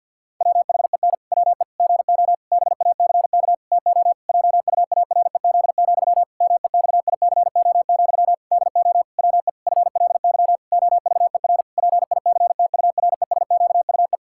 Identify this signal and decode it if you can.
Morse code, heard as a keyed tone, about 52 wpm, 700 Hz. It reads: WHEN WE GO BACK TO JUAREZ, MEXICO, DO WE FLY OVER PICTURESQUE
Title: Pangram Challenge v2 - 50wpm